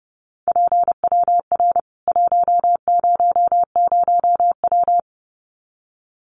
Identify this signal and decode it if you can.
Morse code, heard as a keyed tone, about 30 wpm, 700 Hz. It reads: PWR 100W
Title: Common QSO Phrases 30wpm